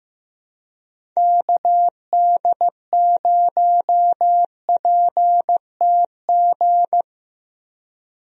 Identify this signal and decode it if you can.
Morse code, heard as a keyed tone, about 15 wpm, 700 Hz. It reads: KD0PTG
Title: USA Call Signs - 6 Characters - Repeated 3x - 15wpm